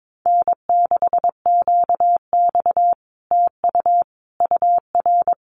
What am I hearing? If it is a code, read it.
Morse code, heard as a keyed tone, about 22 wpm, 700 Hz. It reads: N6QX TU UR
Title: Common QSO Phrases Encore 22wpm